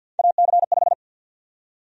Morse code, heard as a keyed tone, about 52 wpm, 700 Hz. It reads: AC5
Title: USA Call Sign Prefixes - 50wpm